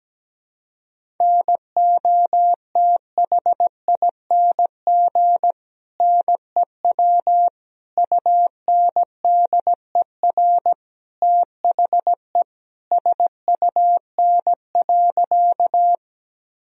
Morse code, heard as a keyed tone, about 17 wpm, 700 Hz. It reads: NOTHING NEW UNDER THE SUN.
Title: Sentences from Top 1000 Words - Review Part 1 - 17wpm